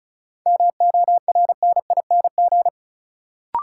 Morse code, heard as a keyed tone, about 36 wpm, 700 Hz. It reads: MORNING
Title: Sentences from Top 400 Words - New Word Drill - 35wpm